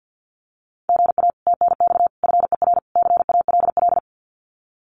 Morse code, heard as a keyed tone, about 50 wpm, 700 Hz. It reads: GA TNX FER CALL
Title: Common QSO Phrases Encore 50wpm